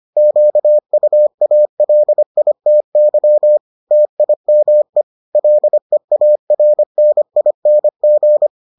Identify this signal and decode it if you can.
Morse code, heard as a keyed tone, about 25 wpm, 600 Hz. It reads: QUALITY TIME LEARNING